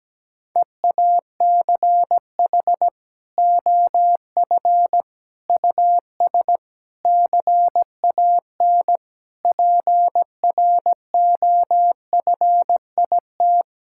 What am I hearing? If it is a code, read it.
Morse code, heard as a keyed tone, about 17 wpm, 700 Hz. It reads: EACH OF US CAN PROFIT
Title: Sentences from Top 1000 Words - Review Part 5 - 17wpm